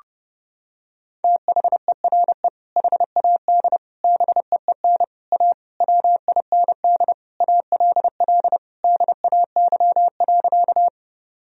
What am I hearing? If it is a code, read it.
Morse code, heard as a keyed tone, about 32 wpm, 700 Hz. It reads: THERE HAD BEEN A WIND ALL DAY.
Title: Sentences from Top 1000 Words - Review Part 2 - 30wpm